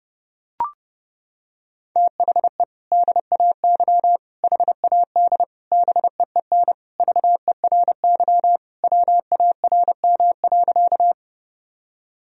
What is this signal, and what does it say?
Morse code, heard as a keyed tone, about 32 wpm, 700 Hz. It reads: THE DAY HAD BEEN VERY WARM.
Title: Sentences from Top 1000 Words - Review Part 2 - 30wpm